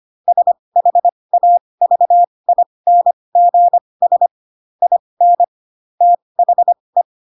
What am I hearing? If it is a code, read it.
Morse code, heard as a keyed tone, about 25 wpm, spaced as for 20 wpm, 700 Hz. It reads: SHAVINGS IN THE
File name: Pinocchio / Pinocchio ch7 15 25 ik1hge